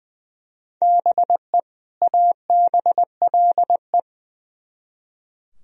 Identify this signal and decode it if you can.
Morse code, heard as a keyed tone, about 23 wpm, 700 Hz. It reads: BE ABLE
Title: Speed-Racing - Sets of 2 Words 20wpm